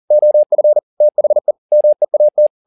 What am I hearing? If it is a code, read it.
Morse code, heard as a keyed tone, about 40 wpm, 600 Hz. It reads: OF THE MEAT
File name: the old man and the sea 2 40wpm